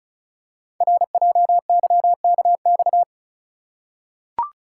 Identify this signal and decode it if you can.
Morse code, heard as a keyed tone, about 36 wpm, 700 Hz. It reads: RJYKX
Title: Code Groups - 5 Letters - 35wpm